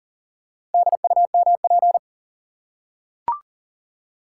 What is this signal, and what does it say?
Morse code, heard as a keyed tone, about 40 wpm, 700 Hz. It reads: DUMP
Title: Four Letter Words - 40wpm